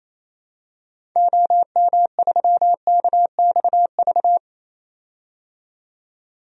Morse code, heard as a keyed tone, about 28 wpm, 700 Hz. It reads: OM3KXV
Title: DXCC-100 Call Signs - 6 Characters - 28wpm